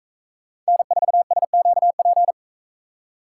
Morse code, heard as a keyed tone, about 42 wpm, 700 Hz. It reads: N4SQP